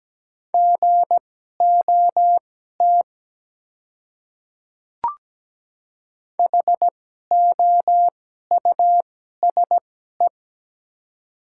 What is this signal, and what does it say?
Morse code, heard as a keyed tone, about 17 wpm, spaced as for 12 wpm, 700 Hz. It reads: GOT HOUSE
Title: 2x Character Spacing - Top 300 Words - 17wpm